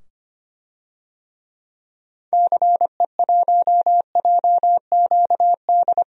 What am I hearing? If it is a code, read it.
Morse code, heard as a keyed tone, about 25 wpm, 700 Hz. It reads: CE1JQD
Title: DXCC-100 Call Signs - 6 Characters - 25wpm